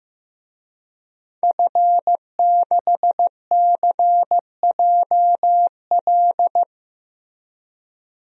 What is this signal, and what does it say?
Morse code, heard as a keyed tone, about 16 wpm, 700 Hz. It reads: F6CJL
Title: DXCC-100 Call Signs - 5 Characters - Repeated 3x - 15wpm